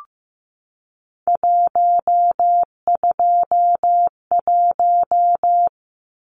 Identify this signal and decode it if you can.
Morse code, heard as a keyed tone, about 15 wpm, 700 Hz. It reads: 121
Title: Numbers - 3-Digit 15wpm